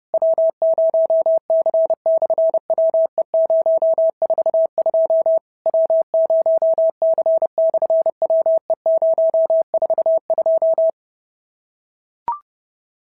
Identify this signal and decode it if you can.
Morse code, heard as a keyed tone, about 30 wpm, 650 Hz. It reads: W0C/WE042 W0C/WE042
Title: SOTA QSO Practice 30wpm